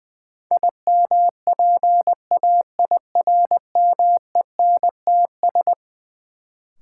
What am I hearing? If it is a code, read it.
Morse code, heard as a keyed tone, about 20 wpm, 700 Hz. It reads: IMPAIRMENTS
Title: Sets of 5 Words Encore - New Word Drill - 20wpm